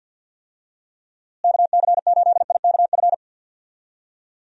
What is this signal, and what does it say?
Morse code, heard as a keyed tone, about 52 wpm, 700 Hz. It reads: KC8IKF
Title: USA Call Signs - 6 Characters - Repeated 3x - 50wpm